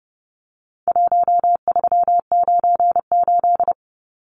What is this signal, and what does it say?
Morse code, heard as a keyed tone, about 30 wpm, 700 Hz. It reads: 1398
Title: Numbers - Contest Style - 4-Digit 30wpm